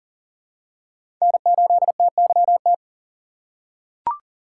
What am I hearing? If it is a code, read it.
Morse code, heard as a keyed tone, about 39 wpm, 700 Hz. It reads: N8TYT
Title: USA Call Signs - 5 Characters - 2x Character Spacing - 40wpm